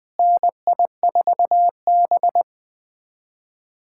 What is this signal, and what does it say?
Morse code, heard as a keyed tone, about 20 wpm, 700 Hz. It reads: NI4B
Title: USA Callsign DE Callsign - 4 Characters - 20wpm